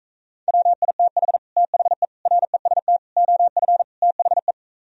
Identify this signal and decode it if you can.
Morse code, heard as a keyed tone, about 45 wpm, 700 Hz. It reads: WITH THE REST OF THE